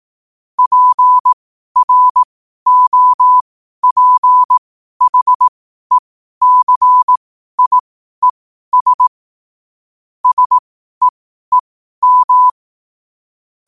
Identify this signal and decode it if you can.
Morse code, heard as a keyed tone, about 18 wpm, spaced as for 12 wpm, 1000 Hz. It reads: PROPHECIES SEEM
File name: qotdfiles / 12wpm